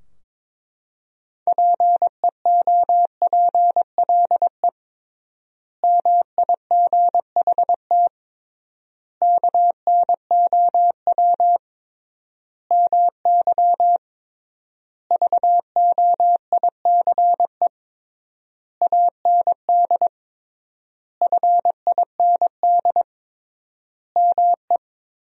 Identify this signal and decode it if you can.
Morse code, heard as a keyed tone, about 22 wpm, 700 Hz. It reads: PEOPLE MIGHT KNOW MY VOICE AND FIND ME
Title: 3x Word-Spacing - Sentences from Top 500 Words - 22wpm